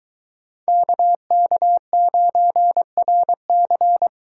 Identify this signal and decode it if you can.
Morse code, heard as a keyed tone, about 21 wpm, 700 Hz. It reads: KK9RC